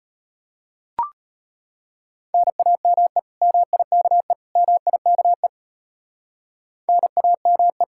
Morse code, heard as a keyed tone, about 33 wpm, 700 Hz. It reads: NAME MIKE MIKE NAME